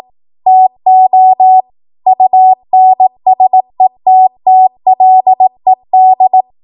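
Morse code, heard as a keyed tone, about 18 wpm, 750 Hz. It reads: TO UNSETTLED
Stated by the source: Code practice at 18 WPM, a transition run bridging 15 to 20 WPM.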